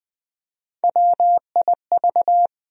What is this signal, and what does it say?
Morse code, heard as a keyed tone, about 21 wpm, 700 Hz. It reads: WIV
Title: USA Call Sign Postfixes - 3 Letters 20wpm